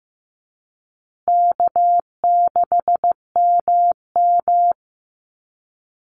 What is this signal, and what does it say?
Morse code, heard as a keyed tone, about 15 wpm, 700 Hz. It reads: K6MM
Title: Easy Call Signs Repeated 3x - 15wpm